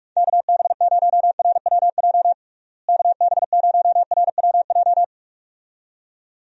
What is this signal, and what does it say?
Morse code, heard as a keyed tone, about 41 wpm, 700 Hz. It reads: KB0RWJ KB0RWJ
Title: USA Call Signs - 6 Characters - Repeated 2x - 45wpm